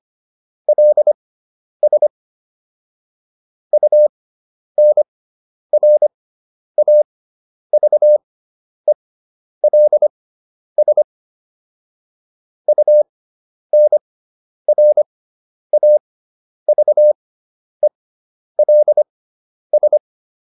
Morse code, heard as a keyed tone, about 25 wpm, spaced as for 10 wpm, 600 Hz. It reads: LS UNRAVELS UNRAVELS